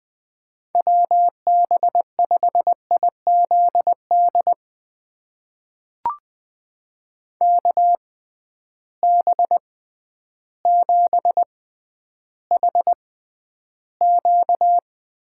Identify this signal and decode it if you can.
Morse code, heard as a keyed tone, about 20 wpm, spaced as for 11 wpm, 700 Hz. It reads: WB5IZD KB7HQ
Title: USA Call Signs - 6 Characters - 6x Character Spacing - 20wpm